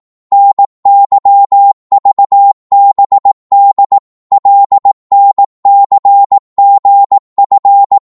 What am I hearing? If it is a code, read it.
Morse code, heard as a keyed tone, about 18 wpm, 800 Hz. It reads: NYVBD LNCGF
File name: Morse / ltr-18wpm